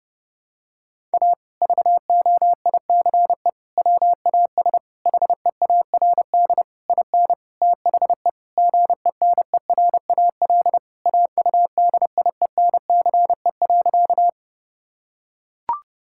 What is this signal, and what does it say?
Morse code, heard as a keyed tone, about 30 wpm, 700 Hz. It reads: A VOICE WAS HEARD IN THE GENERAL AUDIENCE.
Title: Sentences from Top 1000 Words - Review Part 5 - 30wpm